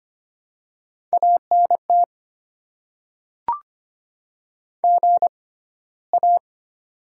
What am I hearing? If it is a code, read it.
Morse code, heard as a keyed tone, about 25 wpm, spaced as for 15 wpm, 700 Hz. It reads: ANT GA
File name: Top 20 QSO Elements - 6x Character Spacing - 25wpm